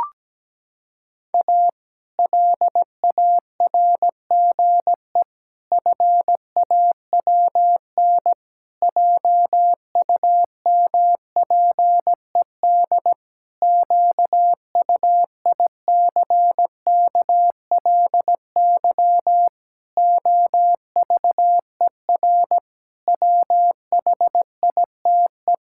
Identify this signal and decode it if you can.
Morse code, heard as a keyed tone, about 17 wpm, 700 Hz. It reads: A LARGE FAWN JUMPED QUICKLY OVER WHITE
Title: Pangram Challenge v2 - 17wpm